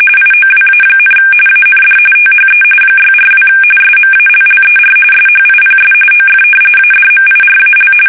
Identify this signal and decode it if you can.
RTTY, about 45.45 baud. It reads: IOWA, AND THE 642 SMALLER SHIPS OF ADMIRAL
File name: fsk850 / battleshipnj